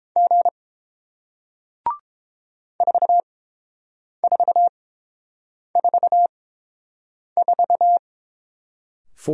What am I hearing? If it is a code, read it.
Morse code, heard as a keyed tone, about 27 wpm, spaced as for 7 wpm, 700 Hz. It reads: G 4444
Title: Speed-Racing Single Letter-Number 22wpm